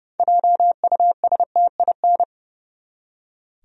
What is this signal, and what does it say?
Morse code, heard as a keyed tone, about 30 wpm, 700 Hz. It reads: JUSTIN
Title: USA Common Names - 30wpm